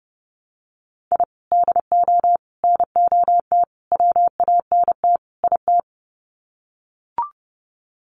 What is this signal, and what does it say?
Morse code, heard as a keyed tone, about 30 wpm, 700 Hz. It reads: I DO NOT WANT IT
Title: Sets of 5 Words Encore2 30wpm